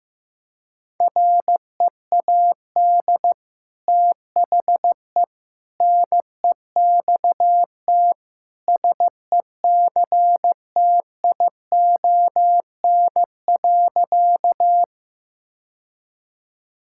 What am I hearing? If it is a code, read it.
Morse code, heard as a keyed tone, about 15 wpm, 700 Hz. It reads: READ THE NEXT SECTION.
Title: Sentences from Top 1000 Words - Review Part 5 - 15wpm